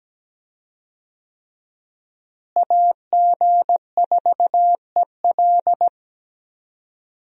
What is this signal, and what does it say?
Morse code, heard as a keyed tone, about 17 wpm, 700 Hz. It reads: AG4EL
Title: USA Call Signs - 5 Characters - Repeated 3x - 17wpm